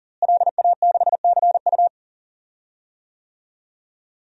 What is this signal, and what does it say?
Morse code, heard as a keyed tone, about 37 wpm, 700 Hz. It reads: LA6CU
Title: DXCC-100 Call Signs - 5 Characters - Repeated 3x - 40wpm